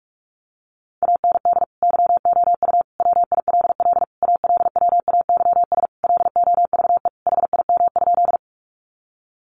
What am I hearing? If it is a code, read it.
Morse code, heard as a keyed tone, about 46 wpm, 700 Hz. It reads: AND YOU WILL ALWAYS LOVE HIM?
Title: Sentences from Top 400 Words 45wpm